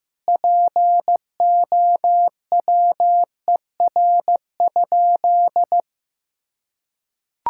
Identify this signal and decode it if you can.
Morse code, heard as a keyed tone, about 15 wpm, 700 Hz. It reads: POWER?
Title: Sentences from Top 1000 Words - Review Part 6 - 15wpm